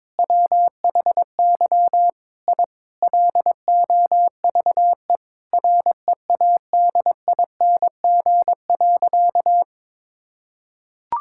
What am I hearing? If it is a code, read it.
Morse code, heard as a keyed tone, about 22 wpm, 700 Hz. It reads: WHY I LOVE READING.
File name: Sentences with 7-Word Phrases - 22wpm